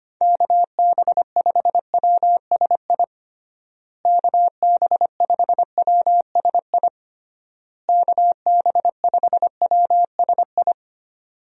Morse code, heard as a keyed tone, about 25 wpm, 700 Hz. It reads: KB5WSI KB5WSI KB5WSI
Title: USA Call Signs - 6 Characters - Repeated 3x - 25wpm